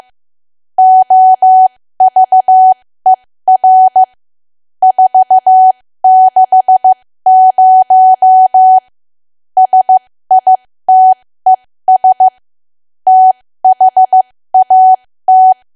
Morse code, heard as a keyed tone, about 15 wpm, spaced as for 13 wpm, 750 Hz. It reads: OVER 460 SITES THAT
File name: cwa / W1AW-Level2 / 13WPM